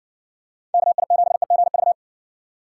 Morse code, heard as a keyed tone, about 60 wpm, 700 Hz. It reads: KI7EGV